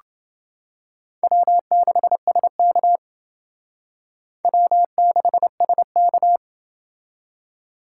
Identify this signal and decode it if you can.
Morse code, heard as a keyed tone, about 28 wpm, 700 Hz. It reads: W6SK W6SK